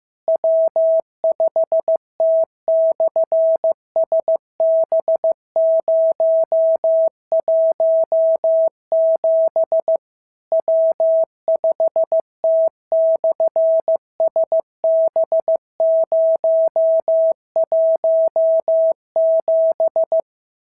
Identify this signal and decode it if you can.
Morse code, heard as a keyed tone, about 15 wpm, 650 Hz. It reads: W5T/SB017 W5T/SB017
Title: SOTA QSO Practice 15wpm